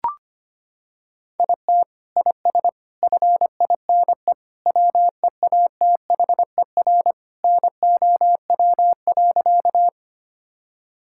Morse code, heard as a keyed tone, about 25 wpm, 700 Hz. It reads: IT IS FINE WEATHER NOW.
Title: Sentences from Top 1000 Words - Review Part 5 - 25wpm